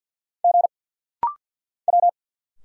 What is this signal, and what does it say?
Morse code, heard as a keyed tone, about 50 wpm, 700 Hz. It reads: G W
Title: Single Letters - Rapid-Fire 50wpm